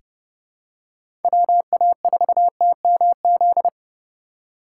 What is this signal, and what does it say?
Morse code, heard as a keyed tone, about 30 wpm, 700 Hz. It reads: WA4TMZ
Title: USA Call Signs - 6 Characters - 4x Character Spacing - 30wpm